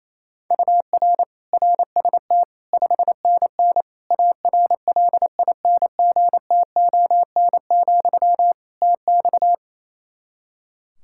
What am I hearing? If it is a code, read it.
Morse code, heard as a keyed tone, about 28 wpm, 700 Hz. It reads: UR RST 5NN ARLINGTON, TX
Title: Practice QSO - Exchange 1 - Beginning - 28wpm